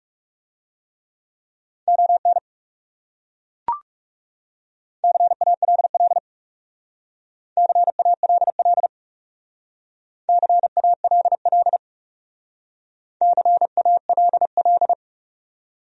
Morse code, heard as a keyed tone, about 38 wpm, 700 Hz. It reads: ON CALL CALL CALL CALL